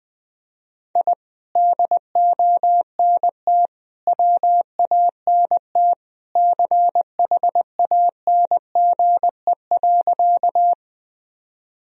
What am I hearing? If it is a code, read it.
Morse code, heard as a keyed tone, about 20 wpm, 700 Hz. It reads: I DONT WANT CHANGE.
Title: Sentences from Top 1000 Words - Review Part 1 - 20wpm